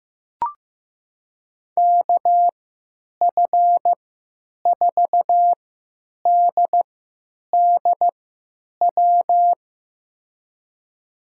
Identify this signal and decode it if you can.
Morse code, heard as a keyed tone, about 15 wpm, spaced as for 9 wpm, 700 Hz. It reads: KF4DDW
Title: USA Call Signs - 6 Characters - 3x Character Spacing - 15wpm